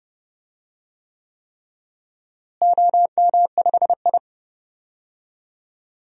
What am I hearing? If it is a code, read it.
Morse code, heard as a keyed tone, about 30 wpm, 700 Hz. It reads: OM5I
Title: DXCC-100 Call Signs - 4 Characters - Repeated 3x - 30wpm